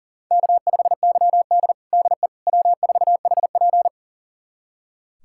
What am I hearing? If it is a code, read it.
Morse code, heard as a keyed tone, about 40 wpm, 700 Hz. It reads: K5YD DE W4HP
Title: USA Callsign DE Callsign - 4 Characters - 40wpm